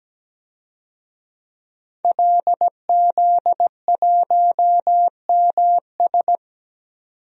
Morse code, heard as a keyed tone, about 17 wpm, 700 Hz. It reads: LZ1MS
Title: DXCC-100 Call Signs - 5 Characters - 17wpm